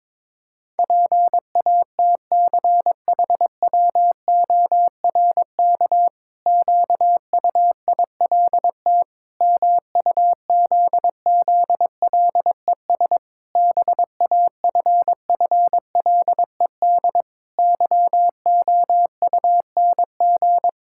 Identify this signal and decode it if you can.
Morse code, heard as a keyed tone, about 22 wpm, 700 Hz. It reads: PATCHWORK QUILT MUZZLES BAFFLED YOUNG
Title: Pangram Challenge v2 - 22wpm